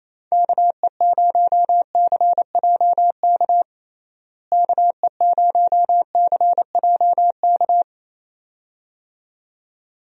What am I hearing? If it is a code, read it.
Morse code, heard as a keyed tone, about 28 wpm, 700 Hz. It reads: KE0CJK KE0CJK
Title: USA Call Signs - 6 Characters - Repeated 2x - 28wpm